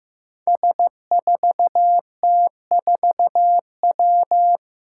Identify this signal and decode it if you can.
Morse code, heard as a keyed tone, about 15 wpm, 700 Hz. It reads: S4T4W
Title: Five Challenging Letter-Number Combinations 15wpm